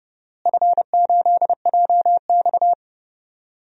Morse code, heard as a keyed tone, about 32 wpm, 700 Hz. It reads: F8JX
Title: DXCC-100 Call Signs - 4 Characters - Repeated 3x - 30wpm